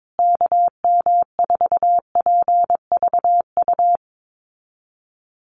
Morse code, heard as a keyed tone, about 22 wpm, 700 Hz. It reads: KM4PVU
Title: USA Call Signs Repeated 3x 22wpm